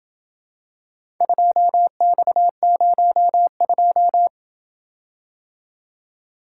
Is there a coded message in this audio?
Yes: Morse code from 1.2 to 4.3 s, heard as a keyed tone, about 28 wpm, 700 Hz. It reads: 2X02